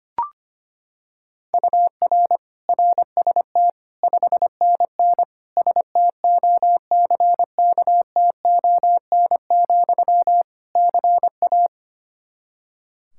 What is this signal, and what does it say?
Morse code, heard as a keyed tone, about 25 wpm, 700 Hz. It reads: UR RST 5NN STOCKTON, CA
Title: K1USN SST QSO Practice 25wpm